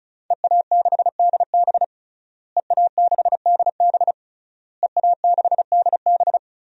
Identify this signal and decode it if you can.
Morse code, heard as a keyed tone, about 36 wpm, 700 Hz. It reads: EA6DB EA6DB EA6DB
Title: DXCC-100 Call Signs - 5 Characters - Repeated 3x - 35wpm